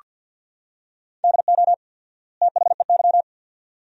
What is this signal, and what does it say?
Morse code, heard as a keyed tone, about 46 wpm, 700 Hz. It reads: DO THEY
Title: 4x Word-Spacing - Sentences from Top 100 Words - 50wpm